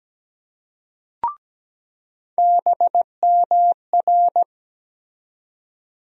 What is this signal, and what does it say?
Morse code, heard as a keyed tone, about 17 wpm, 700 Hz. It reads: BMR
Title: USA Call Sign Postfixes - 3 Letters 17wpm